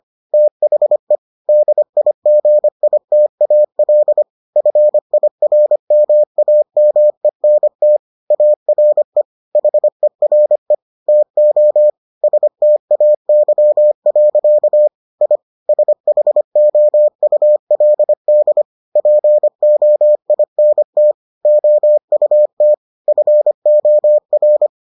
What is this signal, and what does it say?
Morse code, heard as a keyed tone, about 25 wpm, 600 Hz. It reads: THE DIGITAL FIRMAMENT ARE HERE TO STAY. I SHOULD POINT OUT FOR